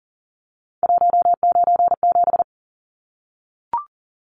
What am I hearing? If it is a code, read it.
Morse code, heard as a keyed tone, about 40 wpm, 700 Hz. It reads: 197
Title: Numbers - 3-Digit 40wpm